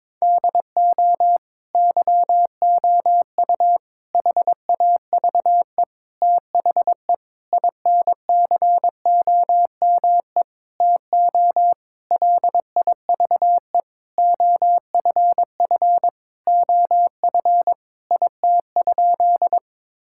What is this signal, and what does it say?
Morse code, heard as a keyed tone, about 22 wpm, 700 Hz. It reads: DO YOU HAVE THE INCOME TO LIVE OFF OF IT?
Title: Sentences from Top 1000 Words - Review Part 5 - 22wpm